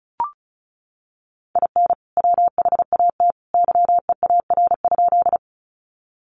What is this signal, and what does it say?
Morse code, heard as a keyed tone, about 34 wpm, 700 Hz. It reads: IN WHAT YEAR?
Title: Sentences from Top 200 Words 35wpm